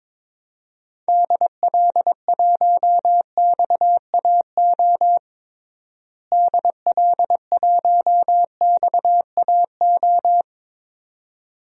Morse code, heard as a keyed tone, about 22 wpm, 700 Hz. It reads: DL1XAO DL1XAO
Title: DXCC-100 Call Signs - 6 Characters - Repeated 2x - 22wpm